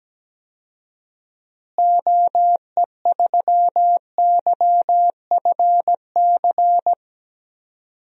Morse code, heard as a keyed tone, about 17 wpm, 700 Hz. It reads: OE3YFC
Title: DXCC-100 Call Signs - 6 Characters - Repeated 2x - 17wpm